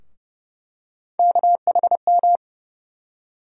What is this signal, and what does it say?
Morse code, heard as a keyed tone, about 30 wpm, 700 Hz. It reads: KHM
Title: USA Call Sign Postfixes - 3 Letters 30wpm